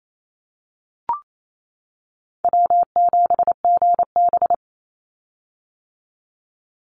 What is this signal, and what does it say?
Morse code, heard as a keyed tone, about 28 wpm, 700 Hz. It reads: W7GB
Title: Easy Call Signs - 28wpm